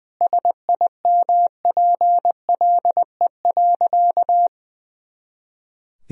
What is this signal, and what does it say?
Morse code, heard as a keyed tone, about 20 wpm, 700 Hz. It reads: SIMPLE.
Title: Sentences from Top 400 Words and Abbreviations 20wpm